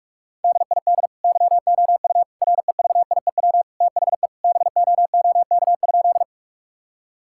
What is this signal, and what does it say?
Morse code, heard as a keyed tone, about 41 wpm, 700 Hz. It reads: DID YOU REVIEW THE BOOK?
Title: Sentences from Top 1000 Words - Review Part 6 - 45wpm